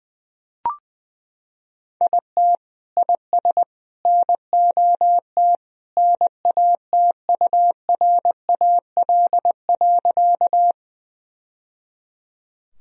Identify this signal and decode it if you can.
Morse code, heard as a keyed tone, about 20 wpm, 700 Hz. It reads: IT IS NOT NATURAL.
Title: Sentences from Top 1000 Words - Review Part 4 - 20wpm